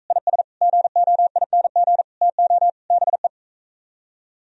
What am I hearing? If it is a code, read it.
Morse code, heard as a keyed tone, about 45 wpm, 700 Hz. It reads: IS GOING TO BE